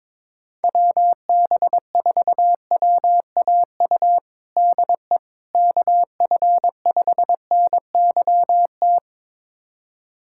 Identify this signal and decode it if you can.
Morse code, heard as a keyed tone, about 22 wpm, 700 Hz. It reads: WB4WAU DE KF5NYT
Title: USA Callsign DE Callsign - 6 Characters - 22wpm